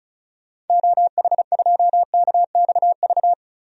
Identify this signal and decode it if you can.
Morse code, heard as a keyed tone, about 34 wpm, 700 Hz. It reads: OH2KXV
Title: DXCC-100 Call Signs - 6 Characters - 35wpm